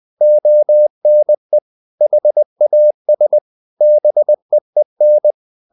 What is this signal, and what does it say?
Morse code, heard as a keyed tone, about 20 wpm, 600 Hz. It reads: ONE HAS BEEN
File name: the old man and the sea part 2 20wpm